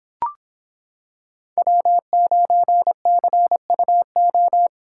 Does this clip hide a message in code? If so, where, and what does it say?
Yes: Morse code from 1.6 to 4.7 s, heard as a keyed tone, about 26 wpm, 700 Hz. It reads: W9CUO